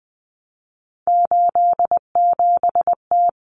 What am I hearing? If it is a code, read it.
Morse code, heard as a keyed tone, about 20 wpm, 700 Hz. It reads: 87T
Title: Common Cut Numbers - 3-Digit 20wpm